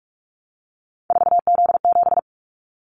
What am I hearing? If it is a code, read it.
Morse code, heard as a keyed tone, about 46 wpm, 700 Hz. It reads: 4Z7
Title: Callsign Prefixes for Top 50 DXCC - 45wpm